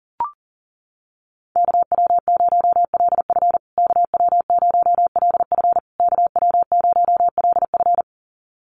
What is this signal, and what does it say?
Morse code, heard as a keyed tone, about 40 wpm, 700 Hz. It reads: KW0LF KW0LF KW0LF
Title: USA Call Signs Repeated 3x 40wpm